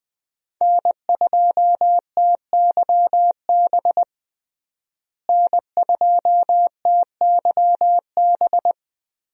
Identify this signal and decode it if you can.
Morse code, heard as a keyed tone, about 20 wpm, 700 Hz. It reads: N2TYB N2TYB
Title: USA Call Signs - 5 Characters - Repeated 3x - 20wpm